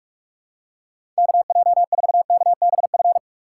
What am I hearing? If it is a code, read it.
Morse code, heard as a keyed tone, about 46 wpm, 690 Hz. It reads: KJ4KBF
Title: USA Call Signs - 3x Character Spacing - 45wpm